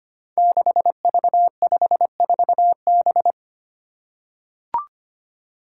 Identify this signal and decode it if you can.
Morse code, heard as a keyed tone, about 25 wpm, 700 Hz. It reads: 6V54B
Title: Five Challenging Letter-Number Combinations 25wpm